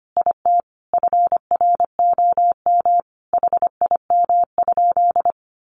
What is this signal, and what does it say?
Morse code, heard as a keyed tone, about 25 wpm, 700 Hz. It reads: IT FROM HIM?
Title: Sentences from Top 1300 Words 25wpm